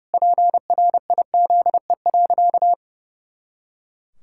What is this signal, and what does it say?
Morse code, heard as a keyed tone, about 30 wpm, 700 Hz. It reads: PRIZE.
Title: Pangram Challenge v2 - 30wpm